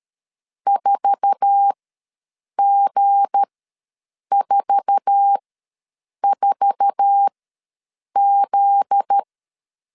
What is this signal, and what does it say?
Morse code, heard as a keyed tone, about 13 wpm, spaced as for 7 wpm, 800 Hz. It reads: 4G44Z
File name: LEZIONI / 8 YJ45 / 8 YJ4 ws 12wpm 3249